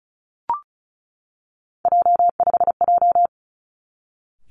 Morse code, heard as a keyed tone, about 36 wpm, 700 Hz. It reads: J5J
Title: Three Challenging Letter-Number Combinations 35wpm